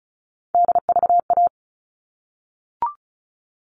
Morse code, heard as a keyed tone, about 34 wpm, 700 Hz. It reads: DVA
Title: All Cut Numbers - 3-Digit 35wpm